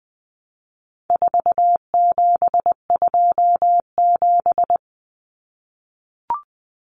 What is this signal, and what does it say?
Morse code, heard as a keyed tone, about 20 wpm, 700 Hz. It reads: 4727
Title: Numbers - 4-Digit 20wpm